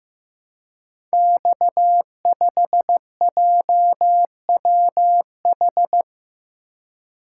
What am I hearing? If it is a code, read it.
Morse code, heard as a keyed tone, about 15 wpm, 700 Hz. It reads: X5JWH
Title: Code Groups - 5 Letter:Numbers - 15wpm